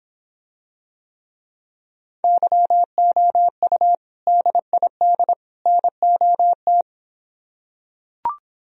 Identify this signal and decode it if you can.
Morse code, heard as a keyed tone, about 26 wpm, 700 Hz. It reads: YOU DID NOT